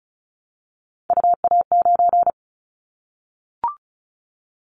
Morse code, heard as a keyed tone, about 34 wpm, 700 Hz. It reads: UA9
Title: Callsign Prefixes for Top 25 DXCC - Repeated 3x - 35wpm